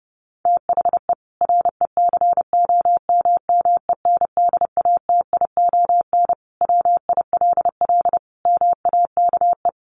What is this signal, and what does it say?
Morse code, heard as a keyed tone, about 30 wpm, 700 Hz. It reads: THE RECOMMENDATION WILL MAKE
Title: Sentences from Top 1400 Words 30wpm